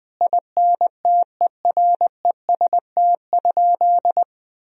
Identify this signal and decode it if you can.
Morse code, heard as a keyed tone, about 20 wpm, 700 Hz. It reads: INTEREST?
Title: Sentences from Top 1000 Words - Review Part 2 - 20wpm